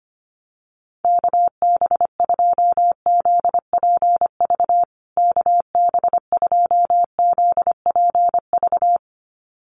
Morse code, heard as a keyed tone, about 25 wpm, 700 Hz. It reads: KB2ZPV KB2ZPV
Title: USA Call Signs Repeated 2x 25wpm